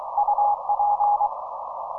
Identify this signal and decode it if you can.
Morse code, heard as a keyed tone, about 24 wpm, 830 Hz, with noise. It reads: UP